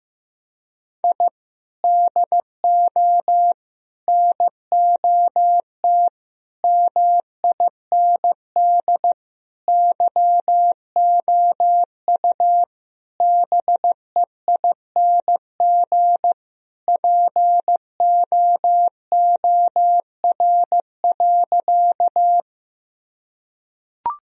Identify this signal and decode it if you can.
Morse code, heard as a keyed tone, about 14 wpm, 700 Hz. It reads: I DO NOT MIND YOU BEING POOR.
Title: Sentences from Top 1000 Words - Review Part 3 - 15wpm